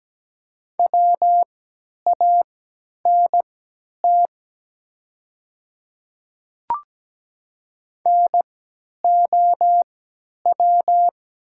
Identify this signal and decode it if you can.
Morse code, heard as a keyed tone, about 17 wpm, spaced as for 10 wpm, 700 Hz. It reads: WANT NOW
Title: 3x Character Spacing - Top 300 Words - 17wpm